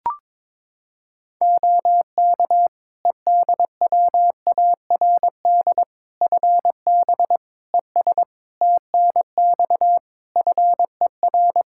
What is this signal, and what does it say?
Morse code, heard as a keyed tone, about 22 wpm, 700 Hz. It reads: OK EDWARD FB ES TNX FER
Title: Practice QSO - Exchange 2 - Info - 22wpm